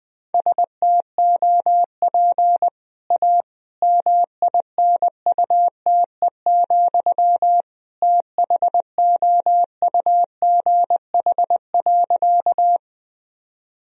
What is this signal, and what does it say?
Morse code, heard as a keyed tone, about 20 wpm, 700 Hz. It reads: STOP A MINUTE, THOUGH.
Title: Sentences from Top 1000 Words - Review Part 2 - 20wpm